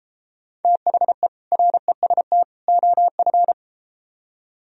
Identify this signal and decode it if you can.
Morse code, heard as a keyed tone, about 32 wpm, 700 Hz. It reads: THE REST OF